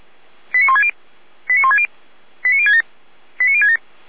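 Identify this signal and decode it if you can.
Five-tone selective call (ZVEI), heard as quick runs of five tones, tones of about 70 ms. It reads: 89160 89160 89067 89067
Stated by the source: ZVEI five-tone call 89223 received by SDR, from feuerwehr-orlamuende.org (fire brigade Orlamünde).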